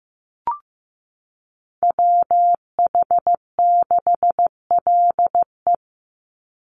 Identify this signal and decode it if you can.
Morse code, heard as a keyed tone, about 15 wpm, 700 Hz. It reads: WH6LE
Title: NA SOTA Activator Callsigns 15wpm